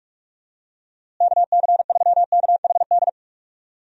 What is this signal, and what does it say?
Morse code, heard as a keyed tone, about 43 wpm, 700 Hz. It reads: KC3KHD
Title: USA Call Signs - 6 Characters - Repeated 2x - 45wpm